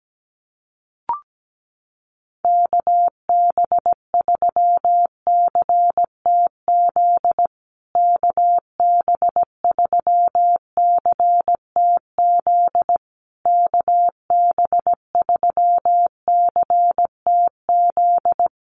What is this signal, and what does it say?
Morse code, heard as a keyed tone, about 17 wpm, 700 Hz. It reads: KB3CTZ KB3CTZ KB3CTZ
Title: USA Call Signs Repeated 3x 17wpm